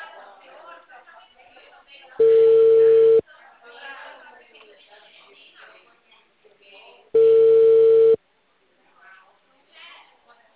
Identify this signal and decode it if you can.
Ringback tone (China, 450 Hz), with speech.
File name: out- -2015- - - .37006